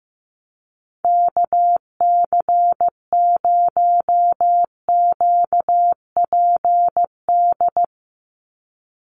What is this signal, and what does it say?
Morse code, heard as a keyed tone, about 15 wpm, 700 Hz. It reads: KC0QPD
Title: USA Call Signs 15wpm